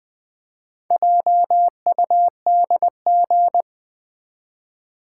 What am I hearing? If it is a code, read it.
Morse code, heard as a keyed tone, about 20 wpm, 700 Hz. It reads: JUDG
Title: Build A Word - 1 Letter At A Time - 5-Letter Words - 20wpm